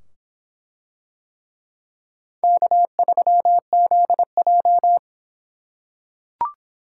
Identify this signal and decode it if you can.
Morse code, heard as a keyed tone, about 24 wpm, 700 Hz. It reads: K3ZJ